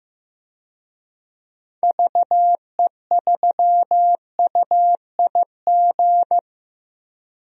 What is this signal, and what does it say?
Morse code, heard as a keyed tone, about 15 wpm, 700 Hz. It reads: VE3UIG
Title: DXCC-100 Call Signs - 6 Characters - 15wpm